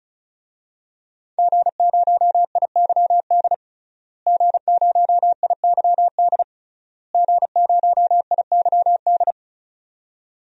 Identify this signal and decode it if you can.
Morse code, heard as a keyed tone, about 34 wpm, 700 Hz. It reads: G0IYD G0IYD G0IYD
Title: DXCC-100 Call Signs - 5 Characters - Repeated 3x - 35wpm